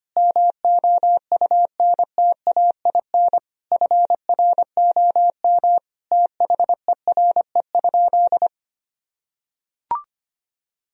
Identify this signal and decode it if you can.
Morse code, heard as a keyed tone, about 25 wpm, 700 Hz. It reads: MOUNTAIN FROM THERE?
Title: Sentences from Top 1000 Words - Review Part 1 - 25wpm